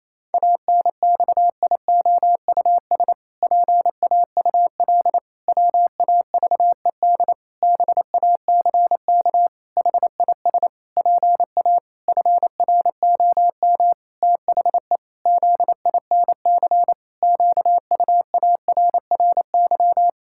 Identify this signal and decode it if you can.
Morse code, heard as a keyed tone, about 28 wpm, 700 Hz. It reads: ANXIOUS PAUL WAVED BACK HIS PA FROM THE ZINC QUARRY
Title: Pangram Challenge v2 - 28wpm